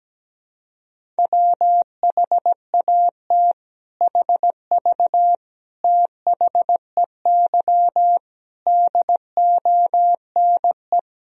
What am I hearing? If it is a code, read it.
Morse code, heard as a keyed tone, about 17 wpm, 700 Hz. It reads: WHAT HV THEY DONE
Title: Sentences from Top 500 Words and Abbreviations 17wpm